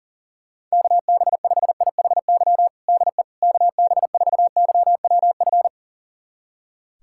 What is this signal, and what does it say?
Morse code, heard as a keyed tone, about 40 wpm, 700 Hz. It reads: KB5IHY DE KB4YWF
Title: USA Callsign DE Callsign - 6 Characters - 40wpm